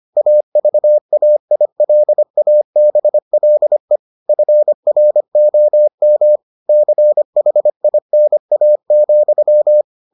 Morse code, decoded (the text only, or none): AVAILABLE FROM CHINA,